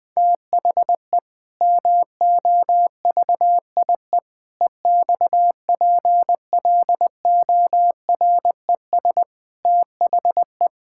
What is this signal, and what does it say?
Morse code, heard as a keyed tone, about 20 wpm, 700 Hz. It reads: THE MOVIE EXPLORES THE
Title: Sentences with 6-Word Phrases - Spoken Beginning - 20wpm